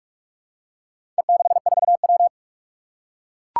Morse code, heard as a keyed tone, about 44 wpm, 700 Hz. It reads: E64W
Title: Four Challenging Letter-Number Combinations 45wpm